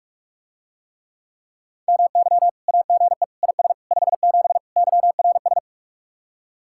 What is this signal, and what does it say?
Morse code, heard as a keyed tone, about 49 wpm, 700 Hz. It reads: MY AGE IS 57 YRS